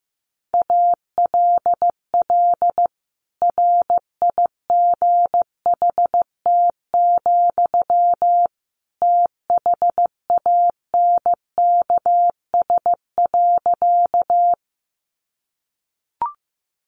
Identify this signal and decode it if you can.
Morse code, heard as a keyed tone, about 15 wpm, 700 Hz. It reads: ALL RIGHT, THANKS.
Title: Sentences from Top 600 Words 15wpm